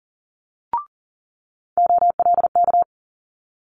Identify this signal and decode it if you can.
Morse code, heard as a keyed tone, about 42 wpm, 700 Hz. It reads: OLK
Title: Multi-letter Phonemes Quick Fire 40wpm